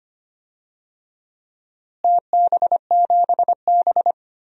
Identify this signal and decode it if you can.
Morse code, heard as a keyed tone, about 25 wpm, 700 Hz. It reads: TB7B
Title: DXCC-100 Call Signs - 4 Characters - Repeated 2x - 25wpm